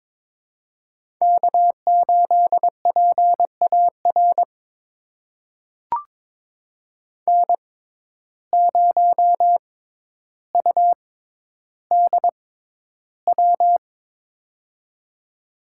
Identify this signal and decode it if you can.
Morse code, heard as a keyed tone, about 22 wpm, spaced as for 11 wpm, 700 Hz. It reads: K8PAR N0UDW
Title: USA Call Signs - 5 Characters - 6x Character Spacing - 22wpm